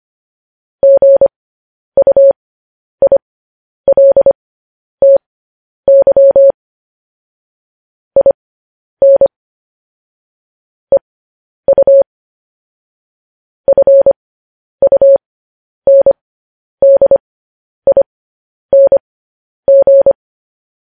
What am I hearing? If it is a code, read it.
Morse code, heard as a keyed tone, about 25 wpm, spaced as for 10 wpm, 550 Hz. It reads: GUILTY IN EU FUNDING
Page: news-headlines-in-morse-code-at-10-wpm